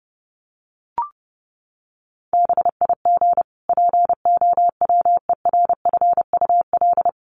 Morse code, heard as a keyed tone, about 30 wpm, 700 Hz. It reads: BIG POWERFUL
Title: Sentences from Top 1300 Words 30wpm